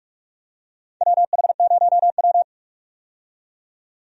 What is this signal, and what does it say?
Morse code, heard as a keyed tone, about 43 wpm, 700 Hz. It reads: WH0W
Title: USA Call Signs - 4 Characters - 45wpm